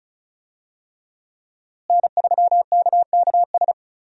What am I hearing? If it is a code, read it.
Morse code, heard as a keyed tone, about 36 wpm, 700 Hz. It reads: N3KKS
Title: USA Call Signs - 5 Characters - Repeated 3x - 35wpm